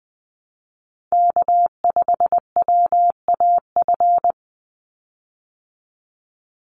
Morse code, heard as a keyed tone, about 20 wpm, 700 Hz. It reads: K5WAF
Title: Call Signs Repeated 2x - 20wpm